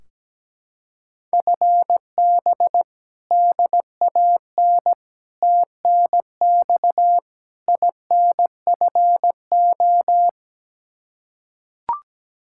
Morse code, heard as a keyed tone, about 17 wpm, 700 Hz. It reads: FB DAN TNX INFO
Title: Practice QSO - Exchange 3 - More Info - 17wpm